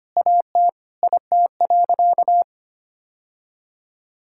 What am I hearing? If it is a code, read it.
Morse code, heard as a keyed tone, about 25 wpm, 700 Hz. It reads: AT IT.
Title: Sentences from Top 1000 Words - Review Part 5 - 25wpm